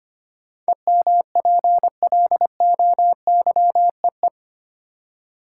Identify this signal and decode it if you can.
Morse code, heard as a keyed tone, about 25 wpm, 700 Hz. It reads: EMPLOYEE
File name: 8 Letter Words - First 3 Letters Given - 25wpm